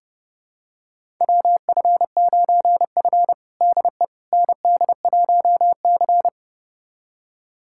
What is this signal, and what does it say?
Morse code, heard as a keyed tone, about 30 wpm, 700 Hz. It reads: WF9F DE ND1C
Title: USA Callsign DE Callsign - 4 Characters - 30wpm